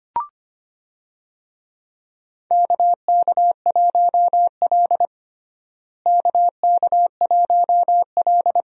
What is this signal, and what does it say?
Morse code, heard as a keyed tone, about 25 wpm, 700 Hz. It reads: KK1L KK1L
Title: USA Call Signs - 4 Characters - Repeated 2x - 25wpm